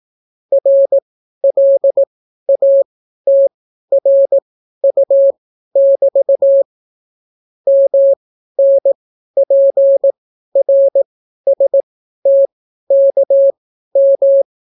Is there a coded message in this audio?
Yes: Morse code from 0.5 to 14.4 s, heard as a keyed tone, about 18 wpm, spaced as for 12 wpm, 550 Hz. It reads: RLATRU<BT> MNPRSTKM